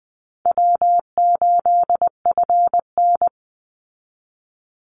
Morse code, heard as a keyed tone, about 20 wpm, 700 Hz. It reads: W8FN
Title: Easy Call Signs Repeated 3x - 20wpm